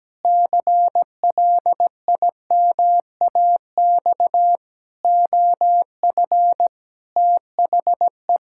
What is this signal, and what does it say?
Morse code, heard as a keyed tone, about 17 wpm, 700 Hz. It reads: CLIMAX OF THE
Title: Pangram Challenge v2 - 17wpm